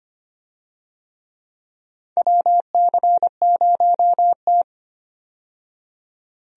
Morse code, heard as a keyed tone, about 25 wpm, 700 Hz. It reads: WC0T
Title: USA Call Signs - 4 Characters - 25wpm